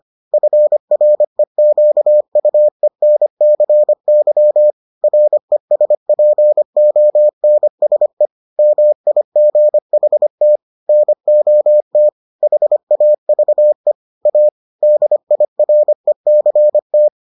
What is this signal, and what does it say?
Morse code, heard as a keyed tone, about 25 wpm, 600 Hz. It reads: FREQUENCY RESPONSE MIGHT NOT HAVE A DIRECT